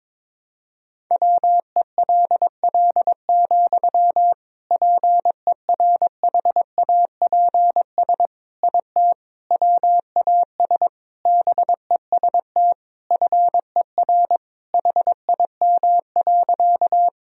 Morse code, heard as a keyed tone, about 22 wpm, 700 Hz. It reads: WELL, PERHAPS IT WAS BEST FER HIM.
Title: Sentences from Top 500 Words and Abbreviations 22wpm